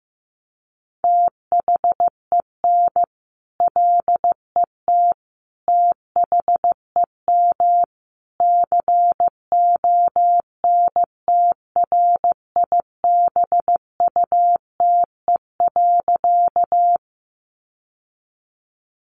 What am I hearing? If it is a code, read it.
Morse code, heard as a keyed tone, about 15 wpm, 700 Hz. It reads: THEN LET THEM CONTRIBUTE.
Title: Sentences from Top 1400 Words 15wpm